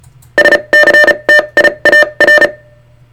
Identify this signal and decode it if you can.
Morse code, heard as a keyed tone, about 34 wpm, spaced as for 27 wpm, 1800 Hz. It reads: SCTIAR
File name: sartechnology / SCTIAR 600hz 35WPM